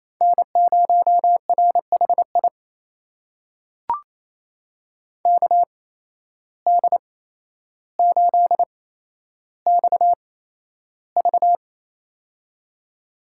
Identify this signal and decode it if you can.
Morse code, heard as a keyed tone, about 28 wpm, spaced as for 12 wpm, 700 Hz. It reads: N0RHI KD8XV
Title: USA Call Signs - 5 Characters - 8x Character Spacing - 28wpm